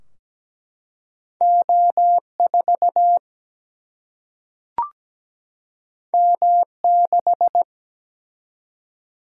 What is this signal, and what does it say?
Morse code, heard as a keyed tone, about 17 wpm, 700 Hz. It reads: O4 M6
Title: Single-Letter Single-Number 17wpm